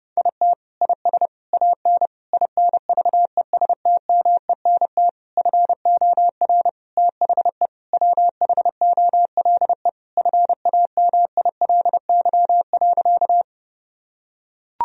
Morse code, heard as a keyed tone, about 30 wpm, 700 Hz. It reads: IT IS AN INVESTMENT FOR THE WHOLE FAMILY.
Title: Sentences from Top 1000 Words - Review Part 4 - 30wpm